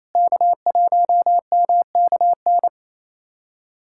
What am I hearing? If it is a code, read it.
Morse code, heard as a keyed tone, about 28 wpm, 700 Hz. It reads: K1MKN
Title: USA Callsign DE Callsign 28wpm